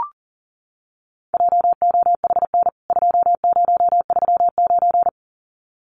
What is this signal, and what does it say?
Morse code, heard as a keyed tone, about 40 wpm, 700 Hz. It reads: JOHN 2039
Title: CWT Exchange 40wpm